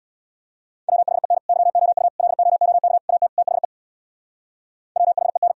Morse code, heard as a keyed tone, about 64 wpm, 700 Hz. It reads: WHEN YOU LOOK AT THE WHEN